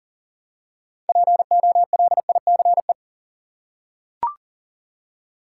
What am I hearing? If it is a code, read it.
Morse code, heard as a keyed tone, about 40 wpm, 700 Hz. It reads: POLICE
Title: Six Letter Words - 40wpm